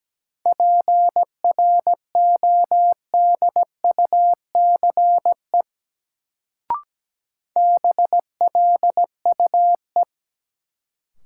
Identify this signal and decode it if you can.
Morse code, heard as a keyed tone, about 17 wpm, 700 Hz. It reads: PRODUCE BLUE
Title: Sentences from Top 1000 Words - Review Part 3 - New Word Drill - 17wpm